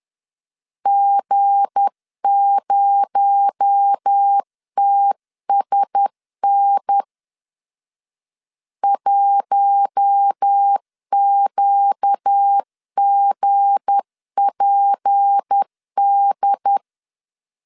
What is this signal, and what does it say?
Morse code, heard as a keyed tone, about 10 wpm, 800 Hz. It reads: G0TSN 1QGPD
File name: LEZIONI / 7 GF9Z / 7 G 10wpm 2501